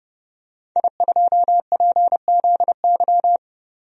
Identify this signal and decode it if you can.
Morse code, heard as a keyed tone, about 32 wpm, 700 Hz. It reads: I2PZY
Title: DXCC-100 Call Signs - 5 Characters - 30wpm